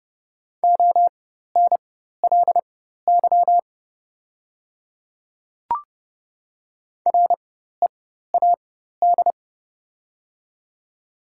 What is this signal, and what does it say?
Morse code, heard as a keyed tone, about 29 wpm, spaced as for 14 wpm, 700 Hz. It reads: ONLY READ
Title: 4x Character Spacing - Top 300 Words - 30wpm